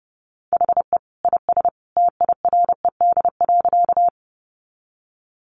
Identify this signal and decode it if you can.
Morse code, heard as a keyed tone, about 30 wpm, 700 Hz. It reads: HE IS TIRED.
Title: Additional Easy Phrases 30wpm